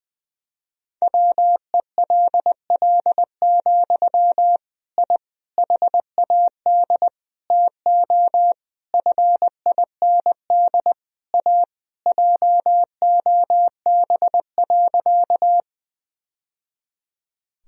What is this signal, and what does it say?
Morse code, heard as a keyed tone, about 21 wpm, 700 Hz. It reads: WELL, I HAD TO FIND A JOB.
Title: Sentences from Top 1000 Words - Review Part 3 - 20wpm